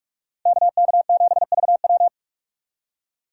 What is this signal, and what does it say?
Morse code, heard as a keyed tone, about 45 wpm, 700 Hz. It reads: KK7VW